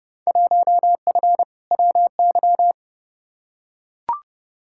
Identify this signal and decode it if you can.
Morse code, heard as a keyed tone, about 32 wpm, 700 Hz. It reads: 1F WY
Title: Field Day Exchanges Encore - 30wpm